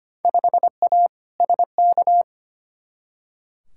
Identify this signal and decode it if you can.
Morse code, heard as a keyed tone, about 25 wpm, 700 Hz. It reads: 5A SK
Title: Field Day Exchanges Encore - 25wpm